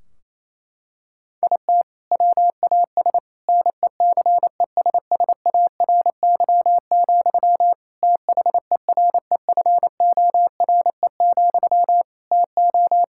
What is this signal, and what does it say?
Morse code, heard as a keyed tone, about 28 wpm, 700 Hz. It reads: IT WAS NECESSARY, THEREFORE, TO
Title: Sentences from Top 1600 Words 28wpm